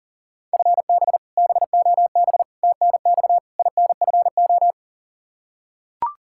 Morse code, heard as a keyed tone, about 40 wpm, 700 Hz. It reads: FB BOB TNX INFO
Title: K1USN SST QSO Practice 40wpm